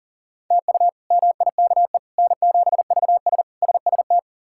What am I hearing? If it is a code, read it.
Morse code, heard as a keyed tone, about 40 wpm, 700 Hz. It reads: TU MIKE N7VS SST
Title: K1USN SST QSO Practice - 40wpm